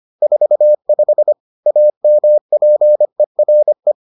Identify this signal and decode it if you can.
Morse code, heard as a keyed tone, about 25 wpm, 600 Hz. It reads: 45 AMPERE